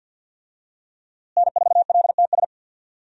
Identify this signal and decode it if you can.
Morse code, heard as a keyed tone, about 51 wpm, 700 Hz. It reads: N4LTS
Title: USA Call Signs - 5 Characters - Repeated 2x - 50wpm